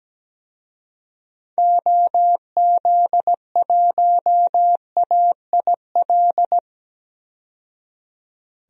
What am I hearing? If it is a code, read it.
Morse code, heard as a keyed tone, about 17 wpm, 700 Hz. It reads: OZ1AIL
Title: DXCC-100 Call Signs - 6 Characters - 17wpm